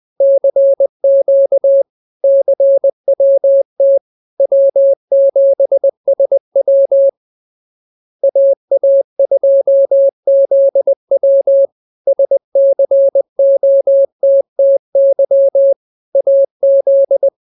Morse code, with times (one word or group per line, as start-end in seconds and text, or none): 0.2-1.8 CQ
2.2-4.0 CWT
4.4-7.1 W7SW
8.2-11.7 AA2ZW
12.1-15.7 SCOTTY
16.1-17.3 AZ